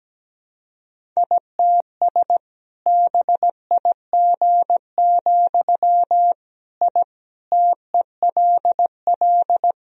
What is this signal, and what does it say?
Morse code, heard as a keyed tone, about 17 wpm, 700 Hz. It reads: ITS BIG, I TELL
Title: Sentences from Top 1000 Words - Review Part 1 - 17wpm